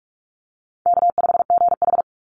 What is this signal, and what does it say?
Morse code, heard as a keyed tone, about 44 wpm, 700 Hz. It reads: K5GH
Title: Call Signs Repeated 3x - 45wpm